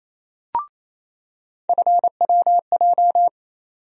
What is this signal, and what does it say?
Morse code, heard as a keyed tone, about 28 wpm, 700 Hz. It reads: FWJ
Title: USA Call Sign Postfixes - 3 Letters 28wpm